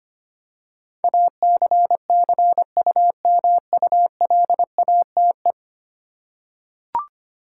Morse code, heard as a keyed tone, about 25 wpm, 700 Hz. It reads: ACCUMULATE
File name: Ten Letter Words - 25wpm